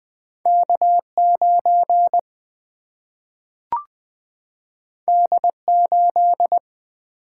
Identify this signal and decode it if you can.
Morse code, heard as a keyed tone, about 20 wpm, 700 Hz. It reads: K9 D8
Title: Single-Letter Single-Number 20wpm